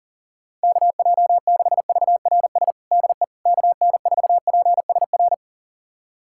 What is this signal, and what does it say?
Morse code, heard as a keyed tone, about 40 wpm, 700 Hz. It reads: KJ6VRS DE KN4PSR
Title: USA Callsign DE Callsign - 6 Characters - 40wpm